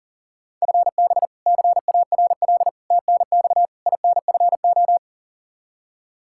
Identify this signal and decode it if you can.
Morse code, heard as a keyed tone, about 40 wpm, 700 Hz. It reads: FB CARL TNX INFO
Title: Practice QSO - Exchange 3 - More Info - 40wpm